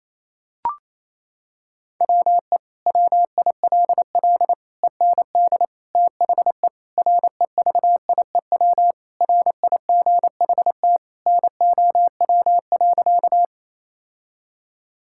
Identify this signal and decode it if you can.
Morse code, heard as a keyed tone, about 27 wpm, 700 Hz. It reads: WE WILL END THE REVIEW RIGHT NOW.
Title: Sentences from Top 1000 Words - Review Part 6 - 28wpm